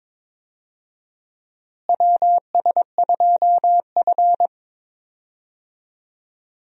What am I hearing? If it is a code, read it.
Morse code, heard as a keyed tone, about 22 wpm, 700 Hz. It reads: WS2F
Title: USA Call Signs - 4 Characters - Repeated 3x - 22wpm